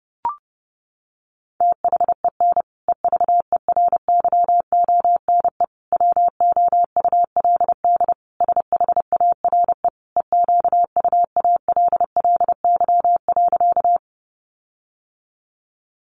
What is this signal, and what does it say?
Morse code, heard as a keyed tone, about 30 wpm, 700 Hz. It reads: THEN EVERYONE WOULD SHARE EQUALLY.
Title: Sentences from Top 1400 Words 30wpm